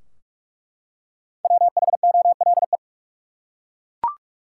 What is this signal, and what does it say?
Morse code, heard as a keyed tone, about 42 wpm, 700 Hz. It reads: WHOLE
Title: Five Letter Words - 45wpm